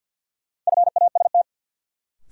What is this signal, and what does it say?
Morse code, heard as a keyed tone, about 53 wpm, 700 Hz. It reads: FAST
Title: Sentences from Top 400 Words - New Word Drill - 50wpm